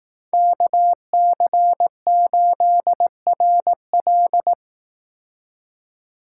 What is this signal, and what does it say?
Morse code, heard as a keyed tone, about 18 wpm, 700 Hz. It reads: KC8RL
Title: Speed Racing - USA 2x2 Call Signs 15wpm